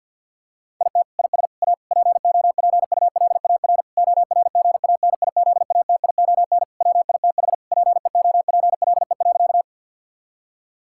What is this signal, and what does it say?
Morse code, heard as a keyed tone, about 50 wpm, 700 Hz. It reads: IT IS A POPULAR ORGANIZATION WITH PEOPLE.
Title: Sentences from Top 1000 Words - Review Part 5 - 50wpm